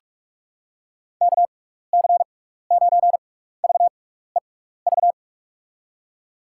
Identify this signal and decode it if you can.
Morse code, heard as a keyed tone, about 44 wpm, spaced as for 15 wpm, 700 Hz. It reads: KC9VEV
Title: USA Call Signs - 6 Characters - 6x Character Spacing - 45wpm